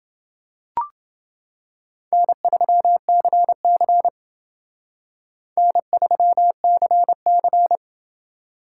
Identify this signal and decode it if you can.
Morse code, heard as a keyed tone, about 28 wpm, 700 Hz. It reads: N3CC N3CC